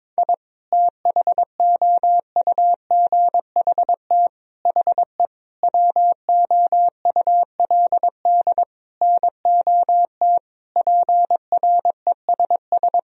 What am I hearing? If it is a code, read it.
Morse code, heard as a keyed tone, about 21 wpm, 700 Hz. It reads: I THOUGHT HE WOULD NOT PRESS
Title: Sentences from Top 1000 Words - Review Part 1 - 22wpm